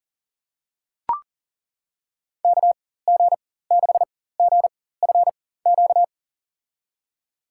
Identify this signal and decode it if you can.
Morse code, heard as a keyed tone, about 40 wpm, spaced as for 19 wpm, 700 Hz. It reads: KG6GFQ
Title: USA Call Signs - 6 Characters - 4x Character Spacing - 40wpm